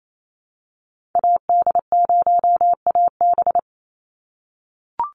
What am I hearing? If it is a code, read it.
Morse code, heard as a keyed tone, about 28 wpm, 700 Hz. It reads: AD0AB
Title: Call Signs - 28wpm